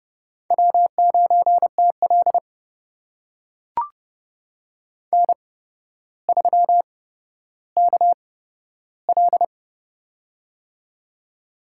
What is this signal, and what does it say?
Morse code, heard as a keyed tone, about 30 wpm, spaced as for 13 wpm, 700 Hz. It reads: W9TL N3KL
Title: USA Call Signs - 4 Characters - 8x Character Spacing - 30wpm